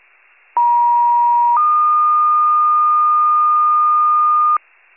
Two-tone page, heard as a one-second tone, then a three-second tone, about 950 then 1220 Hz, with noise. It is heard from 0.6 to 4.6 s.
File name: wcalert